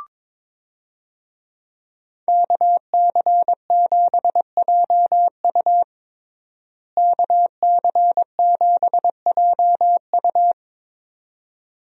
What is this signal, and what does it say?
Morse code, heard as a keyed tone, about 22 wpm, 700 Hz. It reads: KC7JU KC7JU
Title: USA Call Signs - 5 Characters - Repeated 2x - 22wpm